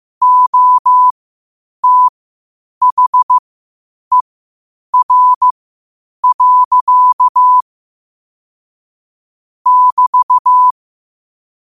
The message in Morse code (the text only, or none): OTHER. <BT>